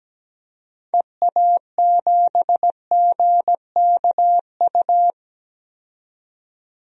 Morse code, heard as a keyed tone, about 17 wpm, 700 Hz. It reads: EA7GKU
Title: DXCC-100 Call Signs - 6 Characters - 17wpm